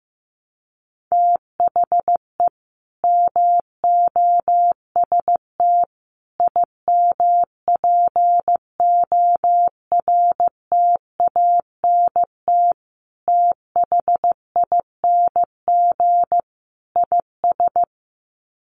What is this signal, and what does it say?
Morse code, heard as a keyed tone, about 15 wpm, 700 Hz. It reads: THE MOST IMPORTANT THING IS
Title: Sets of 5 Words Encore 15wpm